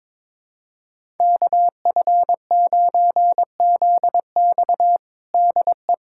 Morse code, heard as a keyed tone, about 22 wpm, 700 Hz. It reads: KF9ZX DE
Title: USA Callsign DE Callsign - 5 Characters - 22wpm